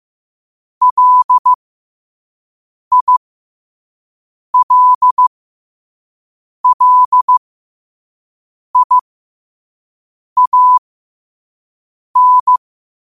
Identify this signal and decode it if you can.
Morse code, heard as a keyed tone, about 15 wpm, spaced as for 5 wpm, 1000 Hz. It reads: LILLIAN
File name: qotdfiles / 5wpm